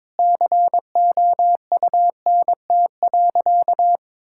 Morse code, heard as a keyed tone, about 21 wpm, 700 Hz. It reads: COUNT.
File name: Difficult Sentences with Hints - 22wpm